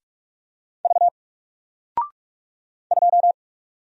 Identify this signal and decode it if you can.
Morse code, heard as a keyed tone, about 44 wpm, 700 Hz. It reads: V 2
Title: Single Letter-Number 45wpm